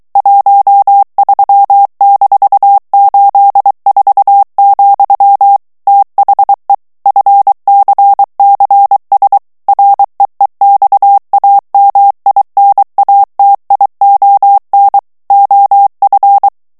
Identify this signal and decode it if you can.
Morse code, heard as a keyed tone, about 23 wpm, 800 Hz. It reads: 13-84, THE FCCS REEXAMINATION OF
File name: ANGLAIS texte clair 3 vitesse 22 mots minute